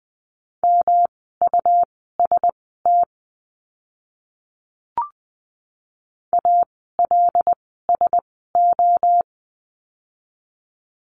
Morse code, heard as a keyed tone, about 20 wpm, spaced as for 14 wpm, 700 Hz. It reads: MUST ALSO
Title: 237 Lesson 38f - Letter q - Words 20wpm